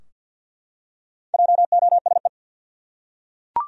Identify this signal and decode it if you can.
Morse code, heard as a keyed tone, about 47 wpm, 700 Hz. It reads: JOSE
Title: USA Common Names - 50wpm